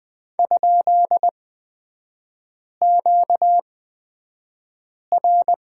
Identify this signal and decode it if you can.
Morse code, heard as a keyed tone, about 20 wpm, spaced as for 5 wpm, 690 Hz. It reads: ?QR
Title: 236 Lesson 38e - Letter q - Three Characters 20wpm